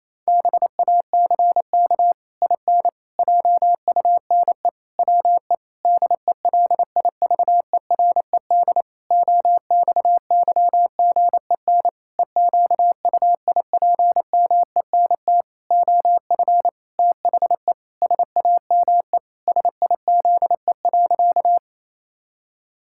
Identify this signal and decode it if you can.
Morse code, heard as a keyed tone, about 28 wpm, 700 Hz. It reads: BACK IN JUNE WE DELIVERED OXYGEN EQUIPMENT OF THE SAME SIZE.
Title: Pangram Challenge v2 - 28wpm